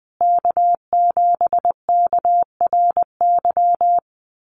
Morse code, heard as a keyed tone, about 19 wpm, 700 Hz. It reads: K7KRY
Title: USA Call Signs Repeated 2x 20wpm